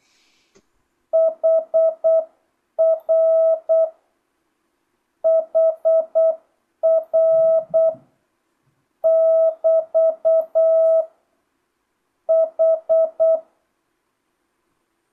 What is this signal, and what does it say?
Morse code, heard as a keyed tone, about 8 wpm, 650 Hz. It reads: HR HR <BT> H
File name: jh8cbh / 2025cw